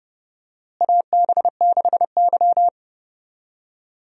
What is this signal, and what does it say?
Morse code, heard as a keyed tone, about 30 wpm, 700 Hz. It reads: AB6Y
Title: USA Call Signs - 4 Characters - Repeated 2x - 30wpm